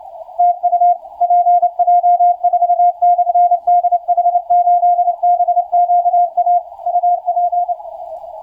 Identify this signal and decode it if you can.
Morse code, heard as a keyed tone, about 29 wpm, 710 Hz. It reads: TU PJ4/DH8BQA UP
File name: PJ4 DH8BQA 12m